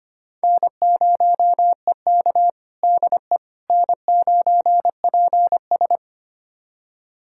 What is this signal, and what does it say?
Morse code, heard as a keyed tone, about 25 wpm, 700 Hz. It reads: N0EK DE N9PS
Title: USA Callsign DE Callsign - 4 Characters - 25wpm